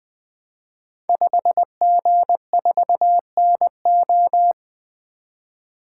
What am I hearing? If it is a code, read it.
Morse code, heard as a keyed tone, about 20 wpm, 700 Hz. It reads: 5G4NO
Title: Code Groups - 5 Letter:Numbers - 20wpm